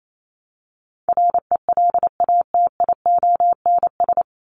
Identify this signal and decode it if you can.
Morse code, heard as a keyed tone, about 28 wpm, 700 Hz. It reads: RELATIONS
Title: ES:S Suffix - Top 100 Words with Base Words 28wpm